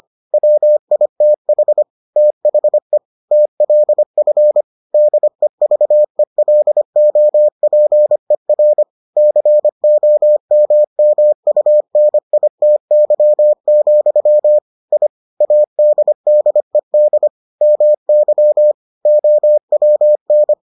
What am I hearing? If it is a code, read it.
Morse code, heard as a keyed tone, about 25 wpm, 600 Hz. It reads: WITH THE TLF DEVELOPER COMMUNITY, I ADDED MY OWN